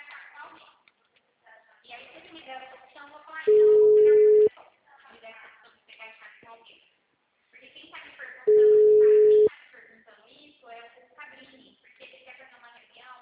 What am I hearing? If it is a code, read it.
Ringback tone (Europe, 425 Hz), with speech.